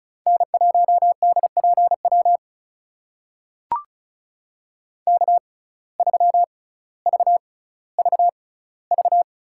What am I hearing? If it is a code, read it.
Morse code, heard as a keyed tone, about 36 wpm, spaced as for 18 wpm, 700 Hz. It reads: N1DPW K3VVV
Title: USA Call Signs - 5 Characters - 6x Character Spacing - 35wpm